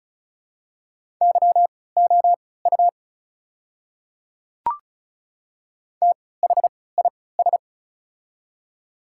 Morse code, heard as a keyed tone, about 36 wpm, spaced as for 20 wpm, 690 Hz. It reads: YOU THIS
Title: 3x Character Spacing - Top 100 Words - 35wpm